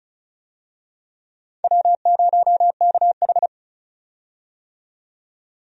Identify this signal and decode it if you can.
Morse code, heard as a keyed tone, about 36 wpm, 700 Hz. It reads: W0KH
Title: USA Call Signs - 4 Characters - 35wpm